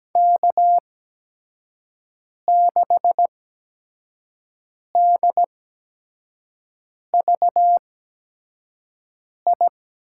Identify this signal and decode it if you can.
Morse code, heard as a keyed tone, about 17 wpm, spaced as for 5 wpm, 700 Hz. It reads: K6DVI
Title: USA Call Signs - 5 Characters - 8x Character Spacing - 17wpm